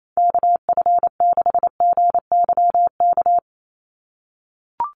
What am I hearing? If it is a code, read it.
Morse code, heard as a keyed tone, about 28 wpm, 700 Hz. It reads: KF6GYK
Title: USA Call Signs 28wpm